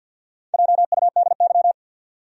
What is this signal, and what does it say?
Morse code, heard as a keyed tone, about 52 wpm, 700 Hz. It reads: JUDY
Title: USA Common Names - Max 4 Letters - 50wpm